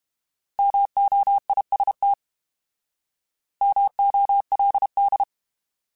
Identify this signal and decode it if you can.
Morse code, heard as a keyed tone, about 32 wpm, 800 Hz. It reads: MOIST MOLD
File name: Alphabetical / Alphabetical mp3s / Ms @30